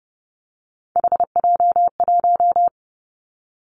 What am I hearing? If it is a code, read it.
Morse code, heard as a keyed tone, about 30 wpm, 700 Hz. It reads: HJ1
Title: Three Challenging Letter-Number Combinations 30wpm